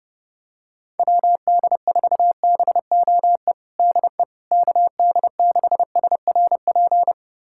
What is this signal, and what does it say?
Morse code, heard as a keyed tone, about 30 wpm, 700 Hz. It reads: WD4BOE DE KD6SRP
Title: USA Callsign DE Callsign 30wpm